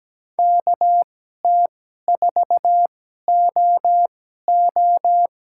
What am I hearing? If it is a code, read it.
Morse code, heard as a keyed tone, about 17 wpm, spaced as for 12 wpm, 700 Hz. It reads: KT4OO
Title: USA Call Signs - 5 Characters - 2x Character Spacing - 17wpm